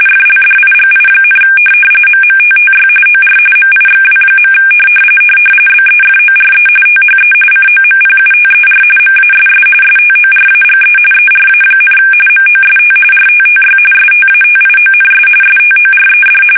RTTY, about 45.45 baud. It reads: FLDIGI, MMTTY OR MIXW. BEFORE THE COMPUTER MASS STORAGE ERA, MOST RTTY STATIONS STORED
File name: fsk850 / teletype-RTTY